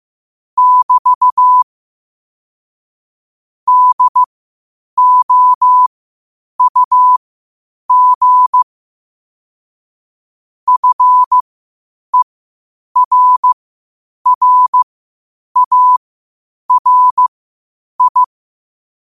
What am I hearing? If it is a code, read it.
Morse code, heard as a keyed tone, about 15 wpm, spaced as for 8 wpm, 1000 Hz. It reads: <BT> DOUG FERRARI
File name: qotdfiles / 8wpm